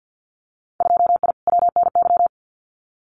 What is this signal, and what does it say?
Morse code, heard as a keyed tone, about 50 wpm, 700 Hz. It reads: 2I WNY
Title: Winter Field Day Exchanges Encore 50wpm